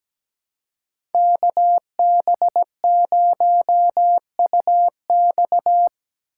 Morse code, heard as a keyed tone, about 17 wpm, 700 Hz. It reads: KB0UX
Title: USA Call Signs - 5 Characters - 3x Character Spacing - 17wpm